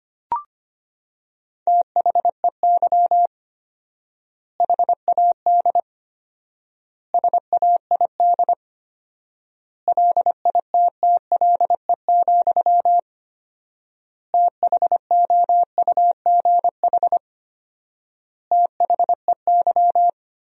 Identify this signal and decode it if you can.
Morse code, heard as a keyed tone, about 25 wpm, 700 Hz. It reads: THEY HAD SAID LITTLE, THOUGH THEY
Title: 4x Word-Spacing - Sentences from Top 400 Words - 25wpm